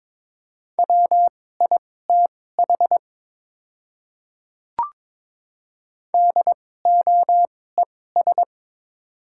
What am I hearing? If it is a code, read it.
Morse code, heard as a keyed tone, about 22 wpm, spaced as for 16 wpm, 700 Hz. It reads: WITH DOES
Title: 2x Character Spacing - Top 200 Words - 22wpm